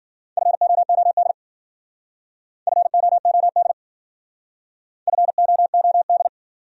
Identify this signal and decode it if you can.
Morse code, heard as a keyed tone, about 54 wpm, 700 Hz. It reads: FOOD FOOD FOOD